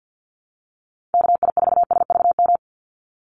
Morse code, heard as a keyed tone, about 50 wpm, 700 Hz. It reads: KI4SVM
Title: NA SOTA Activator Callsigns 50wpm